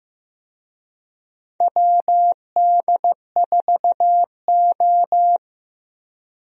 Morse code, heard as a keyed tone, about 15 wpm, 700 Hz. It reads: WD4O
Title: USA Call Signs - 4 Characters - Repeated 3x - 15wpm